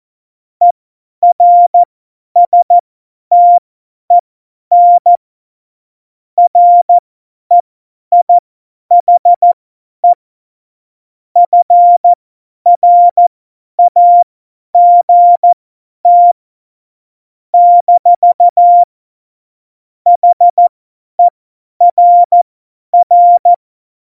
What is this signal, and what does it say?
Morse code, heard as a keyed tone, about 14 wpm, spaced as for 10 wpm, 700 Hz. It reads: ERSTEN REIHE FRAGT - HERR
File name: Morsekurs / Witze / Witz 16. April 2019, 50 F70